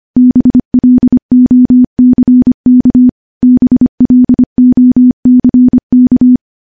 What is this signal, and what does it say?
Morse code, heard as a keyed tone, about 25 wpm, 250 Hz. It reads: BLOCK BLOCK
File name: morse-training-25WPM